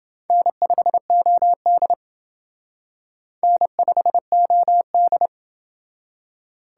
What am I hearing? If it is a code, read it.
Morse code, heard as a keyed tone, about 28 wpm, 700 Hz. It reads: N5OD N5OD